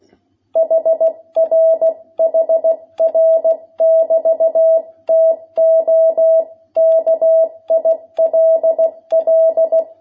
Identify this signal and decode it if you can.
Morse code, heard as a keyed tone, about 16 wpm, 660 Hz. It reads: HR HR <BT> TO KILL
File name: jh8cbh / 2021cw